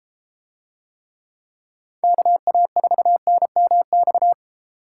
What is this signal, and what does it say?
Morse code, heard as a keyed tone, about 32 wpm, 700 Hz. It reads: KA4NMX